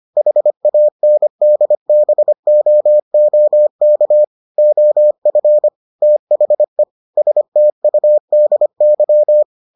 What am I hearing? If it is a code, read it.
Morse code, heard as a keyed tone, about 25 wpm, 600 Hz. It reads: HANDBOOK OF THE STUDY